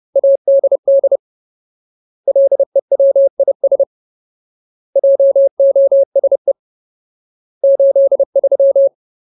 Morse code, heard as a keyed tone, about 30 wpm, 550 Hz. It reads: ADD LEWIS JOSE 83
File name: Ultimate Word List / 30WPM Ultimate